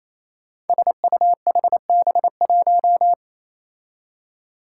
Morse code, heard as a keyed tone, about 28 wpm, 700 Hz. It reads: SUHB1
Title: Five Challenging Letter-Number Combinations 28wpm